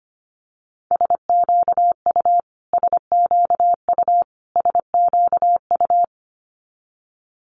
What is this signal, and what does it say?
Morse code, heard as a keyed tone, about 25 wpm, 700 Hz. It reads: SQU SQU SQU
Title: Multi-letter Phonemes Repeated 3x 25wpm